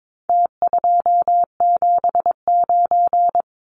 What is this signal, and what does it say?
Morse code, heard as a keyed tone, about 22 wpm, 700 Hz. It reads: T279
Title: Numbers - Contest Style - 4-Digit 22wpm